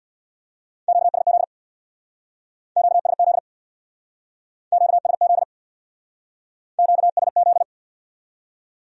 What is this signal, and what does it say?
Morse code, heard as a keyed tone, about 64 wpm, 700 Hz. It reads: QSZ QSZ QSZ QSZ